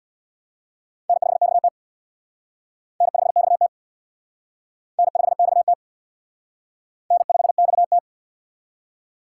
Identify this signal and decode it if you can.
Morse code, heard as a keyed tone, about 61 wpm, 700 Hz. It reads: N5XT N5XT N5XT N5XT
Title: Speed Racing - USA 1x2 Call Signs 50wpm